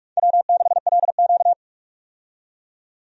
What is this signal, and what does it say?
Morse code, heard as a keyed tone, about 46 wpm, 700 Hz. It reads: W6LQ
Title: USA Call Signs - 4 Characters - 45wpm